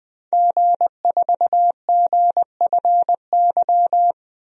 Morse code, heard as a keyed tone, about 20 wpm, 700 Hz. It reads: G4GFY
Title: DXCC-100 Call Signs - 5 Characters - 20wpm